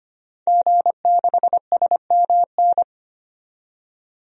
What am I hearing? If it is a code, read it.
Morse code, heard as a keyed tone, about 25 wpm, 700 Hz. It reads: G6SMN
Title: DXCC-100 Call Signs - 5 Characters - 25wpm